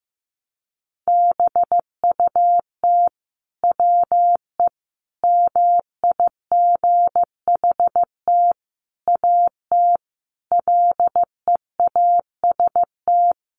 Morse code, heard as a keyed tone, about 15 wpm, 700 Hz. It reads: BUT WE MIGHT AT LEAST
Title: Sentences from Top 1300 Words 15wpm